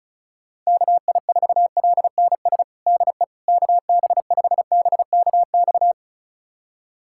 Morse code, heard as a keyed tone, about 36 wpm, 700 Hz. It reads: KI4LNS DE KB5BKX
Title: USA Callsign DE Callsign 35wpm